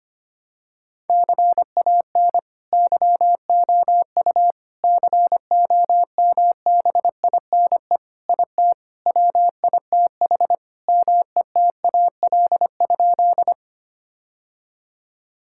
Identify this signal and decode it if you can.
Morse code, heard as a keyed tone, about 25 wpm, 700 Hz. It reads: CAN YOU COMBINE IT WITH METAL?
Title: Sentences from Top 1600 Words 25wpm